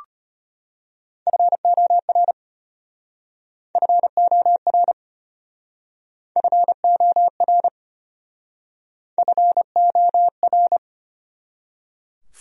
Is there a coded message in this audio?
Yes: Morse code from 1.3 to 10.8 s, heard as a keyed tone, about 32 wpm, 700 Hz. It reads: FOR FOR FOR FOR